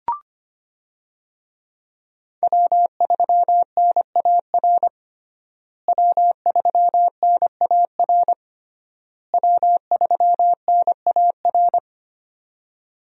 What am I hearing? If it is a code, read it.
Morse code, heard as a keyed tone, about 25 wpm, 700 Hz. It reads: W3NAR W3NAR W3NAR
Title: USA Call Signs - 5 Characters - Repeated 3x - 25wpm